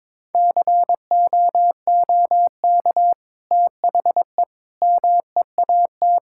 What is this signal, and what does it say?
Morse code, heard as a keyed tone, about 22 wpm, 700 Hz. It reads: COOK THE MEAT
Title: Sentences from Top 1000 Words - Review Part 6 - 22wpm